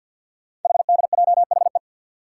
Morse code, heard as a keyed tone, about 49 wpm, 700 Hz. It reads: HDJHE
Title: Code Groups - 5 Letters - 50wpm